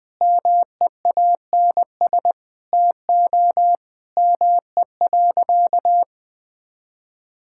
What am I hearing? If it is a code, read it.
Morse code, heard as a keyed tone, about 19 wpm, 700 Hz. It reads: MEANS TO ME.
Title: Sentences with 5-Word Phrases - 20wpm